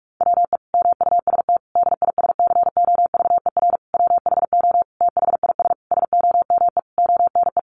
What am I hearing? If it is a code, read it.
Morse code, heard as a keyed tone, about 44 wpm, 700 Hz. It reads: WE MUST DISCOVER WHO THIS SOME ONE
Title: Sentences from Top 1100 Words 45wpm